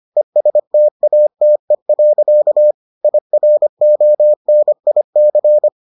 Morse code, heard as a keyed tone, about 25 wpm, 600 Hz. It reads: ESTATE. IRONIC